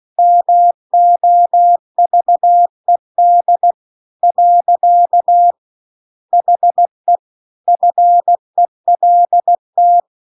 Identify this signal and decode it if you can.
Morse code, heard as a keyed tone, about 16 wpm, 700 Hz. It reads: MOVED . HE FELT